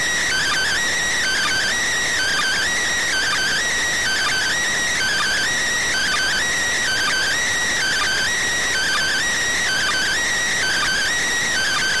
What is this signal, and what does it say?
Two-tone teleprinter signal, about 400 Hz shift.